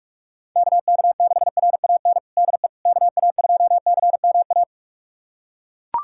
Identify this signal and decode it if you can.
Morse code, heard as a keyed tone, about 41 wpm, 700 Hz. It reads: KK6RAN DE KA2CMA
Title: USA Callsign DE Callsign - 6 Characters - 45wpm